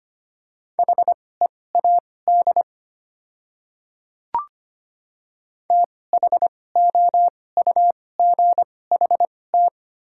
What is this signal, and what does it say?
Morse code, heard as a keyed tone, about 25 wpm, spaced as for 18 wpm, 700 Hz. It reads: HEAD THOUGHT
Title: 2x Character Spacing - Top 300 Words - 25wpm